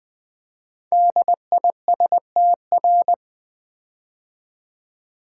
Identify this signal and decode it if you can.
Morse code, heard as a keyed tone, about 20 wpm, 700 Hz. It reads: DISTR
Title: Build A Word - 5 Letters At A Time - 10-Letter Words - 20wpm